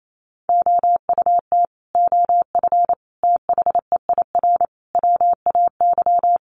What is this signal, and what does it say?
Morse code, heard as a keyed tone, about 28 wpm, 700 Hz. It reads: OUT OF THEIR WAY
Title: Sets of 5 Words Encore2 28wpm